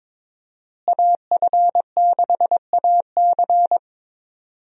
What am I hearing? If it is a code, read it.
Morse code, heard as a keyed tone, about 22 wpm, 700 Hz. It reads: AF6AC
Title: USA Call Signs - 5 Characters - Repeated 3x - 22wpm